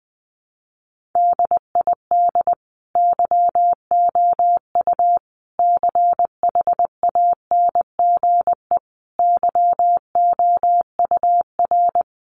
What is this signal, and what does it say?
Morse code, heard as a keyed tone, about 20 wpm, 700 Hz. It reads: DID YOU CHANGE YOUR
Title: Sentences from Top 1400 Words 20wpm